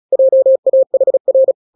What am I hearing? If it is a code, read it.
Morse code, heard as a keyed tone, about 36 wpm, 530 Hz. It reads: JAHR
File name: daily / 35wpm